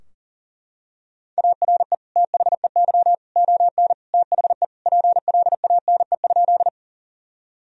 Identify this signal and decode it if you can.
Morse code, heard as a keyed tone, about 40 wpm, 700 Hz. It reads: ARE THEY ON THE PLANE?
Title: Sentences from Top 1000 Words - Review Part 3 - 40wpm